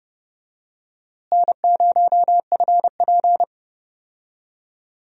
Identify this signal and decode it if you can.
Morse code, heard as a keyed tone, about 30 wpm, 700 Hz. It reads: N0FP
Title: USA Call Signs - 4 Characters - Repeated 3x - 30wpm